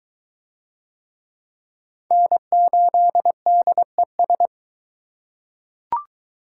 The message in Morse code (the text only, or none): N8DES